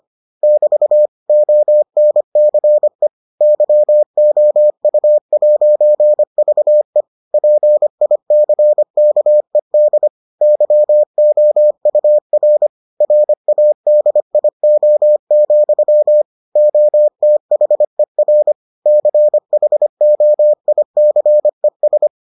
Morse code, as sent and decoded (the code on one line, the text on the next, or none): -...- / --- -. -.-. / -.-- --- ..- .----. ...- . / .--. .. -.-. -.- . -.. / -.-- --- ..- .-. / .-. .- -.. .. --- --..-- / --- - .... .-. / -.-. .... --- .. -.-. ...
<BT> ONCE YOU'VE PICKED YOUR RADIO, OTHER CHOICES